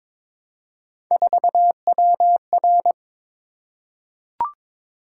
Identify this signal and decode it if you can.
Morse code, heard as a keyed tone, about 23 wpm, 700 Hz. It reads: 4WR
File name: practice / 22wpm / Three Letter-Number Combinations - 22wpm